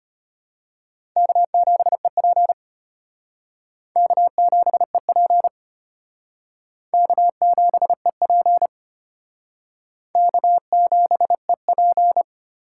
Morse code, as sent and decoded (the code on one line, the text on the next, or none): -.- --... .--. / -.- --... .--. / -.- --... .--. / -.- --... .--.
K7EP K7EP K7EP K7EP